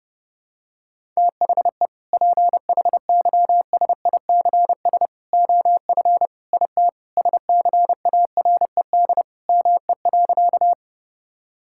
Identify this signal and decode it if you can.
Morse code, heard as a keyed tone, about 30 wpm, 700 Hz. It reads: THE PHYSICS OF IT SCARED ME.
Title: Sentences from Top 1000 Words - Review Part 5 - 30wpm